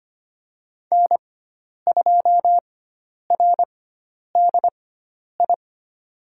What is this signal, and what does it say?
Morse code, heard as a keyed tone, about 25 wpm, spaced as for 10 wpm, 700 Hz. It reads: N2RDI
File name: Lesson 16g - Letter c - Call Signs 25wpm